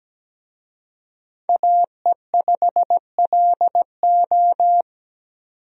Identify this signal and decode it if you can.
Morse code, heard as a keyed tone, about 17 wpm, 700 Hz. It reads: AE5LO
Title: USA Call Signs - 5 Characters - 17wpm